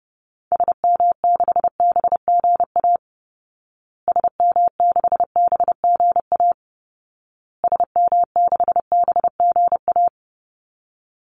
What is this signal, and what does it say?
Morse code, heard as a keyed tone, about 30 wpm, 700 Hz. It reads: SM6BGA SM6BGA SM6BGA
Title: Call Signs Repeated 2x - 30wpm